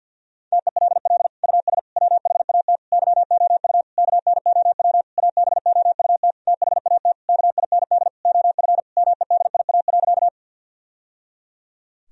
Morse code, heard as a keyed tone, about 50 wpm, 700 Hz. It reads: TELL US WHAT YOU KNOW ABOUT THAT KIND OF MEDIA.
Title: Sentences from Top 1000 Words - Review Part 4 - 50wpm